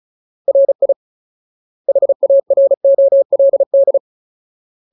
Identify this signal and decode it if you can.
Morse code, heard as a keyed tone, about 34 wpm, 550 Hz. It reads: RI HAROLD
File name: Ultimate Word List / 35WPM Ultimate